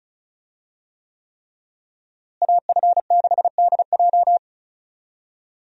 Morse code, heard as a keyed tone, about 36 wpm, 700 Hz. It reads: AF6DJ
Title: USA Call Signs - 5 Characters - Repeated 3x - 35wpm